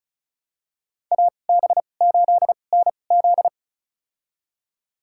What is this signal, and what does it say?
Morse code, heard as a keyed tone, about 36 wpm, spaced as for 25 wpm, 700 Hz. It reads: AB8NZ
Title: USA Call Signs - 5 Characters - 2x Character Spacing - 35wpm